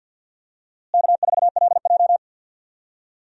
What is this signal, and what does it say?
Morse code, heard as a keyed tone, about 48 wpm, 700 Hz. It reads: K4LJ
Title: USA Call Signs - 4 Characters - Repeated 3x - 50wpm